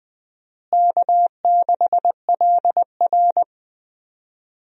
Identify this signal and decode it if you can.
Morse code, heard as a keyed tone, about 20 wpm, 700 Hz. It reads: K6LR
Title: USA Call Signs - 4 Characters - Repeated 3x - 20wpm